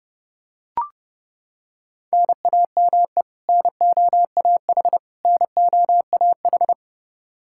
Morse code, heard as a keyed tone, about 30 wpm, 700 Hz. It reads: NAME NOAH NOAH
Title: Practice QSO - Exchange 1 - Beginning - 30wpm